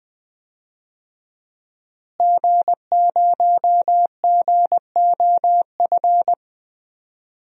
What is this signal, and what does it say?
Morse code, heard as a keyed tone, about 20 wpm, 700 Hz. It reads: G0GOF
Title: DXCC-100 Call Signs - 5 Characters - Repeated 2x - 20wpm